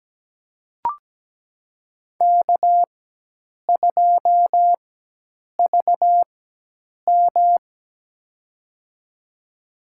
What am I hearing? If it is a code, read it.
Morse code, heard as a keyed tone, about 17 wpm, spaced as for 8 wpm, 700 Hz. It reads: K2VM
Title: USA Call Signs - 4 Characters - 4x Character Spacing - 17wpm